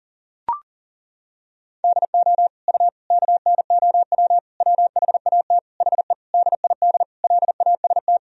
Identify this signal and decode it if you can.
Morse code, heard as a keyed tone, about 40 wpm, 700 Hz. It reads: DO U KNOW WHAT HE DID LAST
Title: Sentences from Top 300 Words and Abbreviations 40wpm